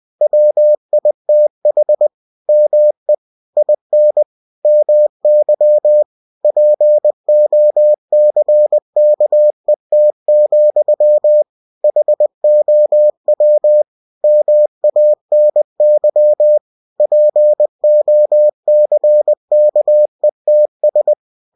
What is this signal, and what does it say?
Morse code, heard as a keyed tone, about 20 wpm, 600 Hz. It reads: WITH ME IN MY POCKET, HOW MANY POCKETS
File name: cw-books / winnie-the-pooh / winnie-20wpm-page-0060